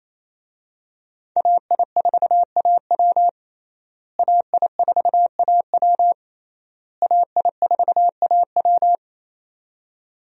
Morse code, heard as a keyed tone, about 28 wpm, 700 Hz. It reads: AI4AW AI4AW AI4AW
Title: USA Call Signs - 5 Characters - Repeated 3x - 28wpm